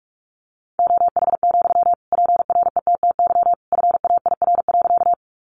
Morse code, heard as a keyed tone, about 44 wpm, 700 Hz. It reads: OH, PRETTY FAIR.
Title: Sentences from Top 1300 Words 45wpm